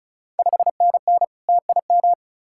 Morse code, heard as a keyed tone, about 36 wpm, 690 Hz. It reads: 5NN TIM
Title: Ontario QSO Party Exchange 35wpm